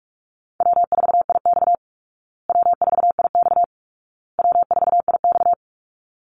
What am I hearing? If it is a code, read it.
Morse code, heard as a keyed tone, about 44 wpm, 700 Hz. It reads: W4IX W4IX W4IX
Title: Call Signs Repeated 2x - 45wpm